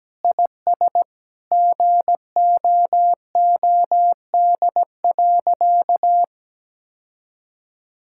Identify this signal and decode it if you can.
Morse code, heard as a keyed tone, about 17 wpm, 700 Hz. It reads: IS GOOD.
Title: Sentences from Top 1000 Words - Review Part 6 - 17wpm